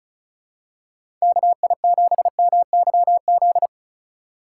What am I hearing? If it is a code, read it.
Morse code, heard as a keyed tone, about 34 wpm, 700 Hz. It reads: KI7MYZ
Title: USA Call Signs - 6 Characters - 4x Character Spacing - 35wpm